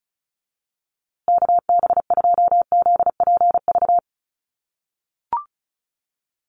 Morse code, heard as a keyed tone, about 34 wpm, 700 Hz. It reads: KB2ZPV
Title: USA Call Signs Repeated 2x 35wpm